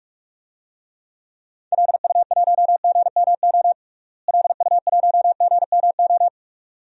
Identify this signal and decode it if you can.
Morse code, heard as a keyed tone, about 47 wpm, 700 Hz. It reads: LU1GMO LU1GMO
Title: DXCC-100 Call Signs - 6 Characters - Repeated 2x - 45wpm